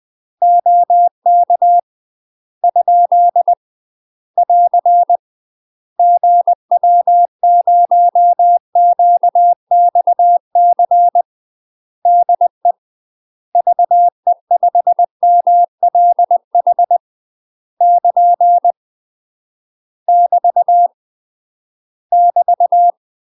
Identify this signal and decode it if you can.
Morse code, heard as a keyed tone, about 20 wpm, 700 Hz. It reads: OK ? <AR> GW0QXC DE VE5MLH <KN> <BT> <BT>
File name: UK MASTER QSOs TRAINING / 20WPM / UK MASTER QSOs 0058